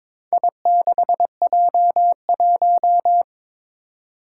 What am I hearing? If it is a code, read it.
Morse code, heard as a keyed tone, about 22 wpm, 700 Hz. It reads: I6J1
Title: Four Challenging Letter-Number Combinations 22wpm